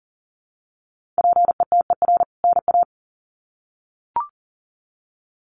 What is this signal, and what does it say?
Morse code, heard as a keyed tone, about 40 wpm, 700 Hz. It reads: PETER NA
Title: CWT Exchange 40wpm